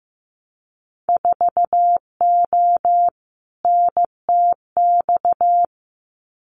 Morse code, heard as a keyed tone, about 15 wpm, 700 Hz. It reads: 4O NTX
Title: Winter Field Day Exchanges 15wpm